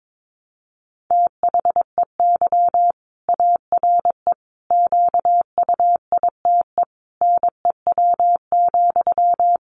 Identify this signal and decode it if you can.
Morse code, heard as a keyed tone, about 22 wpm, 700 Hz. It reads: THEY ARE QUITE NEW,
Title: Sentences from Top 700 Words 22wpm